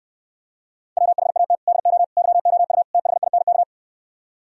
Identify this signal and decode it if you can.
Morse code, heard as a keyed tone, about 69 wpm, 700 Hz. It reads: WHAT DO YOU THINK